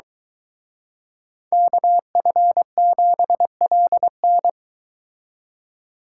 Morse code, heard as a keyed tone, about 23 wpm, 700 Hz. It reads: KF7LN